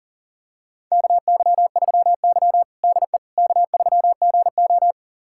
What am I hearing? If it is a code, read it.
Morse code, heard as a keyed tone, about 40 wpm, 700 Hz. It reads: KY3Y DE K3GO
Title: USA Callsign DE Callsign - 4 Characters - 40wpm